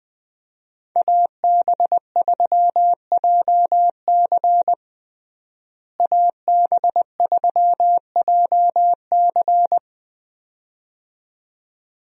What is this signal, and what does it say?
Morse code, heard as a keyed tone, about 20 wpm, 700 Hz. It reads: AB3JC AB3JC
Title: USA Call Signs - 5 Characters - Repeated 2x - 20wpm